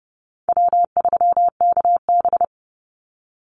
Morse code, heard as a keyed tone, about 30 wpm, 700 Hz. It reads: W3KB
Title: Easy Call Signs - 30wpm